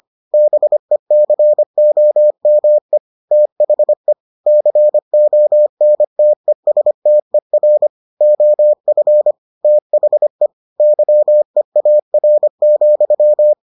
Morse code, heard as a keyed tone, about 25 wpm, 600 Hz. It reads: BECOME THE CONTESTER OF THE YEAR,